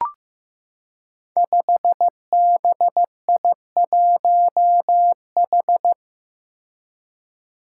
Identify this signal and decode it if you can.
Morse code, heard as a keyed tone, about 15 wpm, 700 Hz. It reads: 5BI1H
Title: Five Challenging Letter-Number Combinations 15wpm